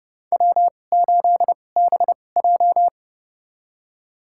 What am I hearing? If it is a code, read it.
Morse code, heard as a keyed tone, about 31 wpm, spaced as for 22 wpm, 700 Hz. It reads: W8BJ
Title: USA Call Signs - 2x Character Spacing - 30wpm